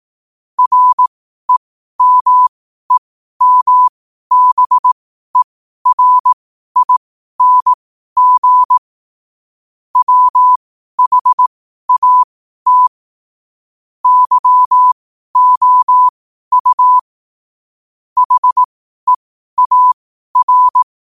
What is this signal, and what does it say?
Morse code, heard as a keyed tone, about 18 wpm, spaced as for 12 wpm, 1000 Hz. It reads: REMEMBERING WHAT YOU HEAR
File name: qotdfiles / 12wpm